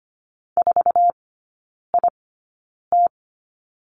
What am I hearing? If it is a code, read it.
Morse code, heard as a keyed tone, about 25 wpm, spaced as for 9 wpm, 700 Hz. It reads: 4IT
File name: Lesson 17e - Letter u - Three Characters 25wpm